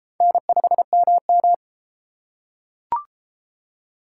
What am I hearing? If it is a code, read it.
Morse code, heard as a keyed tone, about 31 wpm, 700 Hz. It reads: N5MM